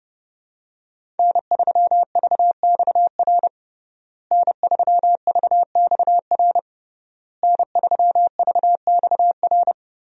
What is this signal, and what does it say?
Morse code, heard as a keyed tone, about 29 wpm, 700 Hz. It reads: N3VXR N3VXR N3VXR
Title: USA Call Signs - 5 Characters - Repeated 3x - 30wpm